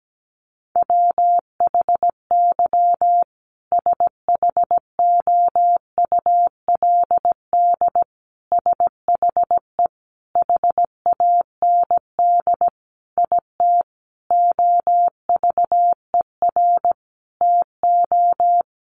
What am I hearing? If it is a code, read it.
Morse code, heard as a keyed tone, about 17 wpm, 700 Hz. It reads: WHY SHOULD SHE HAND IT OVER TO
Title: Sentences from Top 900 Words 17wpm